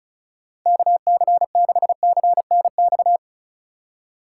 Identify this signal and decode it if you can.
Morse code, heard as a keyed tone, about 36 wpm, 700 Hz. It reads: KC6CNX
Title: USA Call Signs - 6 Characters - 3x Character Spacing - 35wpm